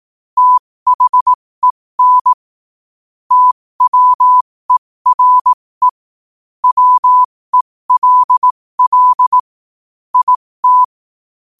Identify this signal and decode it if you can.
Morse code, heard as a keyed tone, about 18 wpm, spaced as for 15 wpm, 1000 Hz. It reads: THEN TWERE WELL IT
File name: qotdfiles / 15wpm